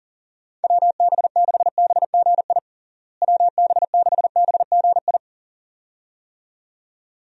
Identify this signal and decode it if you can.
Morse code, heard as a keyed tone, about 40 wpm, 700 Hz. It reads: WB6BGI WB6BGI
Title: USA Call Signs - 6 Characters - Repeated 2x - 40wpm